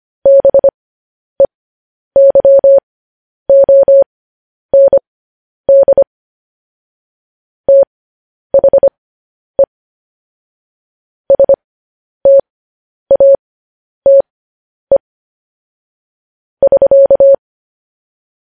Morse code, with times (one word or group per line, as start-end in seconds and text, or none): 0.3-6.0 BEYOND
7.7-9.6 THE
11.3-15.0 STATE
16.6-17.3 <SK>